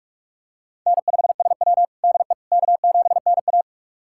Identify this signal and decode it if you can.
Morse code, heard as a keyed tone, about 46 wpm, 700 Hz. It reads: N5SW DE K7NA
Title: USA Callsign DE Callsign - 4 Characters - 45wpm